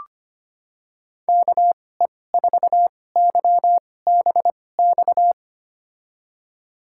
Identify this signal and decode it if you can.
Morse code, heard as a keyed tone, about 25 wpm, spaced as for 18 wpm, 700 Hz. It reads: KE4YBX
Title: USA Call Signs - 6 Characters - 2x Character Spacing - 25wpm